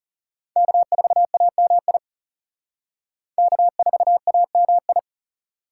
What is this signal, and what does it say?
Morse code, heard as a keyed tone, about 36 wpm, 700 Hz. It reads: K4AMI K4AMI
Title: Speed Racing - USA 1x3 Call Signs 30wpm